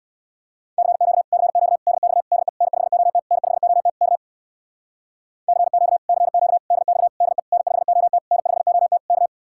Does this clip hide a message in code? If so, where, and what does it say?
Yes: Morse code from 0.8 to 9.3 s, heard as a keyed tone, about 66 wpm, 700 Hz. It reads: CQ CQ DX DE N5OT N5OT K CQ CQ DX DE N5OT N5OT K